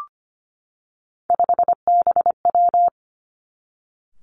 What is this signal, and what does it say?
Morse code, heard as a keyed tone, about 25 wpm, 700 Hz. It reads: 5BW
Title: Three Challenging Letter-Number Combinations 25wpm